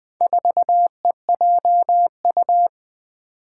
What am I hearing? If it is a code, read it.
Morse code, heard as a keyed tone, about 20 wpm, 700 Hz. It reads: 4EJU
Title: Four Challenging Letter-Number Combinations 20wpm